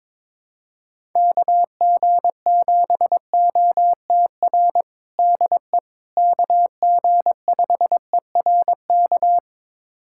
Morse code, heard as a keyed tone, about 22 wpm, 700 Hz. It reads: KG7OTR DE KG5ERK
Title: USA Callsign DE Callsign - 6 Characters - 22wpm